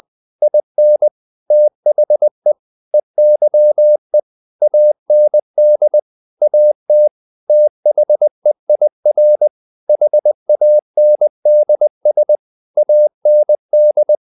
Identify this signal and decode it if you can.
Morse code, heard as a keyed tone, about 20 wpm, 600 Hz. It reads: IN THE EYE AND AT THEIR HANDS AND
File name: the old man and the sea part 2 20wpm